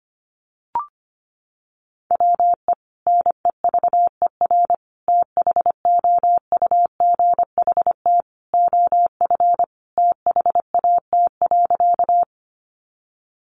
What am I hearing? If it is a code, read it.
Morse code, heard as a keyed tone, about 25 wpm, 700 Hz. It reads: WE NEVER THOUGHT OF THAT.
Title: Sentences from Top 300 Words 25wpm